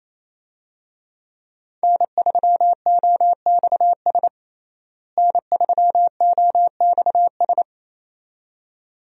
Morse code, heard as a keyed tone, about 28 wpm, 700 Hz. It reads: N3OXS N3OXS
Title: USA Call Signs - 5 Characters - Repeated 2x - 28wpm